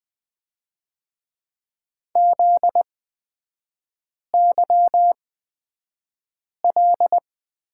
Morse code, heard as a keyed tone, about 20 wpm, spaced as for 5 wpm, 700 Hz. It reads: ZYL
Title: 243 Lesson 39e - Letter z - Three Characters 20wpm